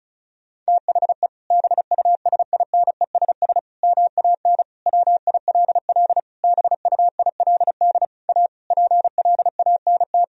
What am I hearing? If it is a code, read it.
Morse code, heard as a keyed tone, about 34 wpm, 700 Hz. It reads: THE BUSINESS MAN WILL BUILD A PLANT
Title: Sentences from Top 1000 Words - Review Part 1 - 35wpm